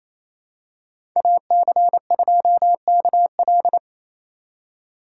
Morse code, heard as a keyed tone, about 27 wpm, 700 Hz. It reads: AC2KL
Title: USA Call Signs - 4x Character Spacing - 28wpm